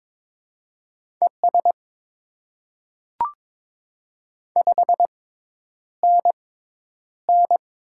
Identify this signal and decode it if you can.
Morse code, heard as a keyed tone, about 22 wpm, spaced as for 10 wpm, 700 Hz. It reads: ES 5NN
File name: Top 20 QSO Elements - 6x Character Spacing - 22wpm